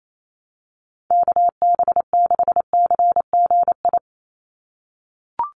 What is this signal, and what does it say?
Morse code, heard as a keyed tone, about 28 wpm, 700 Hz. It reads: KB6CGI
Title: USA Call Signs 28wpm